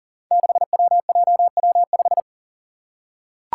Morse code, heard as a keyed tone, about 40 wpm, 700 Hz. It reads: 6WJW5
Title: Five Challenging Letter-Number Combinations 40wpm